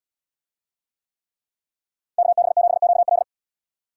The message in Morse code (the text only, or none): KB7QL